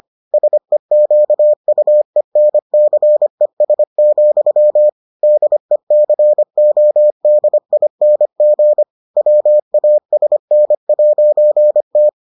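Morse code, heard as a keyed tone, about 25 wpm, 600 Hz. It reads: SEQUENCES, DECODING WASN'T
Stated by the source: Morse edition of the Foundations of Amateur Radio episodes.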